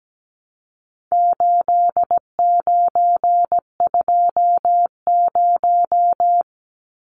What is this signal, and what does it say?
Morse code, heard as a keyed tone, about 17 wpm, 700 Hz. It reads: 8920
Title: Numbers - 4-Digit 17wpm